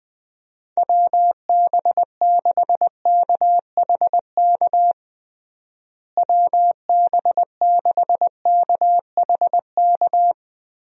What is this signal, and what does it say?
Morse code, heard as a keyed tone, about 21 wpm, 700 Hz. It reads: WB6KHK WB6KHK
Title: USA Call Signs - 6 Characters - Repeated 3x - 20wpm